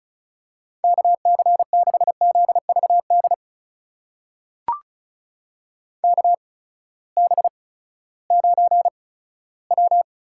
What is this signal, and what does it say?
Morse code, heard as a keyed tone, about 36 wpm, spaced as for 18 wpm, 700 Hz. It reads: KC6ZVD KB9W
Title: USA Call Signs - 6 Characters - 8x Character Spacing - 35wpm